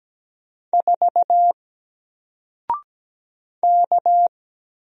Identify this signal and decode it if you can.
Morse code, heard as a keyed tone, about 17 wpm, 700 Hz. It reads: 4 K
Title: Single Letter-Number Alt 17wpm